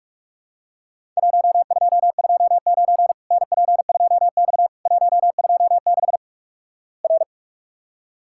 Morse code, heard as a keyed tone, about 46 wpm, 700 Hz. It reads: 1229 NP2X 126 R
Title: Worked All Europe DX Contest - QTC Exchange - 45wpm